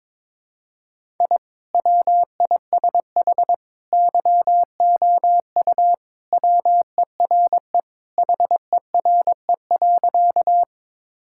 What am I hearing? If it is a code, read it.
Morse code, heard as a keyed tone, about 22 wpm, 700 Hz. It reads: I WISH YOU WERE HERE.
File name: practice / 22wpm / Sentences with 2-Word Phrases - 22wpm